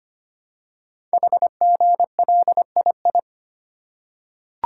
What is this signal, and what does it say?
Morse code, heard as a keyed tone, about 25 wpm, 700 Hz. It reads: HGLII
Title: Code Groups - 5 Letters - 25wpm